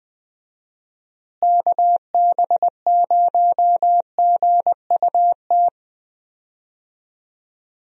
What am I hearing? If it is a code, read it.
Morse code, heard as a keyed tone, about 20 wpm, 700 Hz. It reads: KB0GUT
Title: USA Call Signs - 6 Characters - Repeated 3x - 20wpm